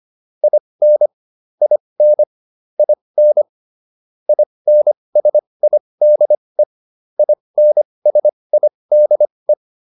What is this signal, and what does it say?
Morse code, heard as a keyed tone, about 25 wpm, spaced as for 20 wpm, 600 Hz. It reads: IN IN IN INSIDE INSIDE